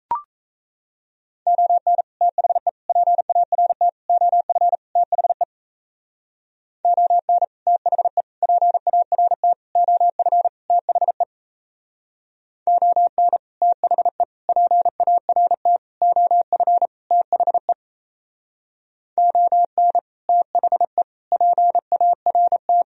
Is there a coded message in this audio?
Yes: Morse code from 1.5 to 22.8 s, heard as a keyed tone, about 35 wpm, 700 Hz. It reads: ON THE PART OF THE ON THE PART OF THE ON THE PART OF THE ON THE PART